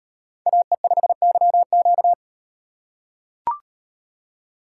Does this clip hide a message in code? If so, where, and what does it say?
Yes: Morse code from 0.5 to 2.1 s, heard as a keyed tone, about 37 wpm, 700 Hz. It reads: AE5YQ